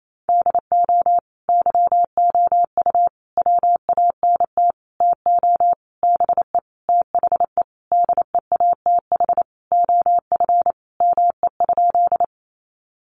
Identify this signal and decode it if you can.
Morse code, heard as a keyed tone, about 28 wpm, 700 Hz. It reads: DO YOU WANT TO BE THE DEATH OF ME?
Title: Sentences from Top 1100 Words 28wpm